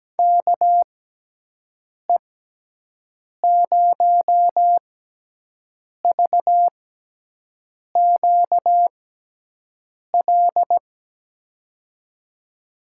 Morse code, heard as a keyed tone, about 17 wpm, spaced as for 6 wpm, 700 Hz. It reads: KE0VQL
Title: USA Call Signs - 6x Character Spacing - 17wpm